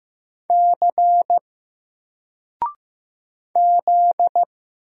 Morse code, heard as a keyed tone, about 15 wpm, 700 Hz. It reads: C Z
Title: Single Character (Letter:Number:Punctuation:Prosign) 15wpm